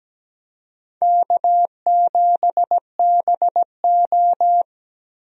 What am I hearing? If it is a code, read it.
Morse code, heard as a keyed tone, about 17 wpm, 700 Hz. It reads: K7BO
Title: USA Call Signs - 4 Characters - Repeated 3x - 17wpm